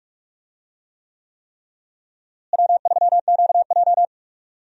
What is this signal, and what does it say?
Morse code, heard as a keyed tone, about 46 wpm, 700 Hz. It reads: W3QJ
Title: USA Call Signs - 4 Characters - Repeated 3x - 45wpm